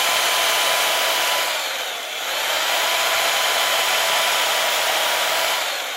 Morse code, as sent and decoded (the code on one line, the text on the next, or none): none